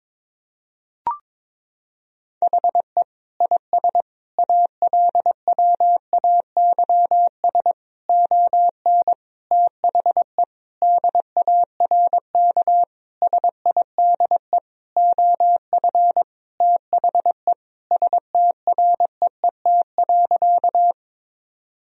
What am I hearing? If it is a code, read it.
Morse code, heard as a keyed tone, about 22 wpm, 700 Hz. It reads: HE IS ALWAYS ON THE DARK SIDE OF THE STREET.
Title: Sentences from Top 1000 Words - Review Part 3 - 22wpm